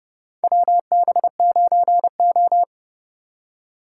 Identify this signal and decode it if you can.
Morse code, heard as a keyed tone, about 30 wpm, 700 Hz. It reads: WB9O
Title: USA Call Signs - 4 Characters - 3x Character Spacing - 30wpm